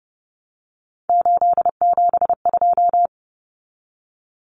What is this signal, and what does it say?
Morse code, heard as a keyed tone, about 30 wpm, 700 Hz. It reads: 872
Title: Common Cut Numbers - 3-Digit 30wpm